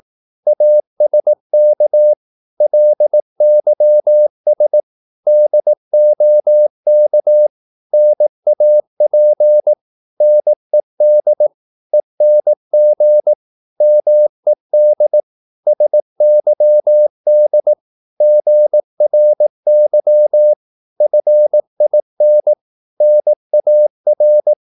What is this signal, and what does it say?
Morse code, heard as a keyed tone, about 18 wpm, 600 Hz. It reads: ASK LYS DOK NAP NED ENG MED SYD GRY FIN NAR
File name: Korte ord DK 18wpm